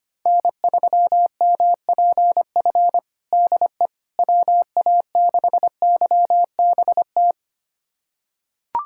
Morse code, heard as a keyed tone, about 25 wpm, 700 Hz. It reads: N3MPF DE WA6YBT
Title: USA Callsign DE Callsign 25wpm